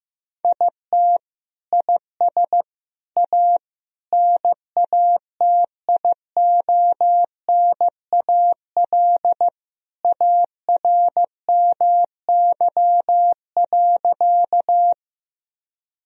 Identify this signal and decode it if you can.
Morse code, heard as a keyed tone, about 15 wpm, 700 Hz. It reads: IT IS A NATIONAL ARMY.
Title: Sentences from Top 1000 Words - Review Part 4 - 15wpm